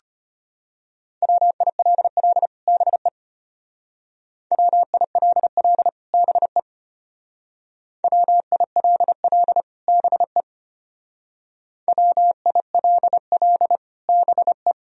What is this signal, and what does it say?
Morse code, heard as a keyed tone, about 33 wpm, 700 Hz. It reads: WILL BE WILL BE WILL BE WILL BE